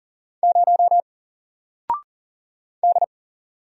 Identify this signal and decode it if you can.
Morse code, heard as a keyed tone, about 40 wpm, 700 Hz. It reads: 0 D
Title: Single Letter-Number Alt 40wpm